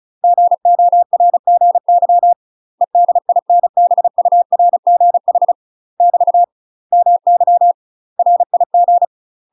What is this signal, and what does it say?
Morse code, heard as a keyed tone, about 36 wpm, 700 Hz. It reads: GORGY EDINBURGH <BT> MY RIG
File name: UK MASTER QSOs TRAINING / 35WPM / UK MASTER QSOs 0032